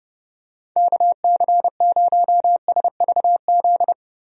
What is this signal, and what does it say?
Morse code, heard as a keyed tone, about 30 wpm, 700 Hz. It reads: KC0SVZ
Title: USA Call Signs - 6 Characters - Repeated 3x - 30wpm